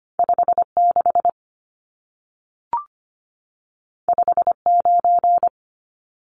Morse code, heard as a keyed tone, about 25 wpm, 700 Hz. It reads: 56 59
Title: Numbers - 2-Digit 25wpm